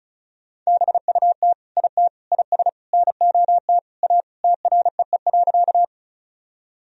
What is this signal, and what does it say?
Morse code, heard as a keyed tone, about 34 wpm, 700 Hz. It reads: BUT IT IS NOT A TREE.
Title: Sentences from Top 1000 Words - Review Part 1 - 35wpm